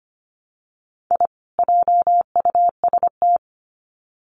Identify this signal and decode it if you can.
Morse code, heard as a keyed tone, about 25 wpm, 700 Hz. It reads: I JUST
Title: Sets of 2 Words 25wpm